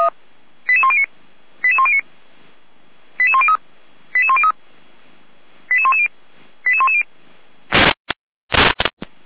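Five-tone selective call (ZVEI), heard as quick runs of five tones, tones of about 70 ms. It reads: 88109 88109 88103 88103 88110 88110
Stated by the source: ZVEI five-tone call received on SDR.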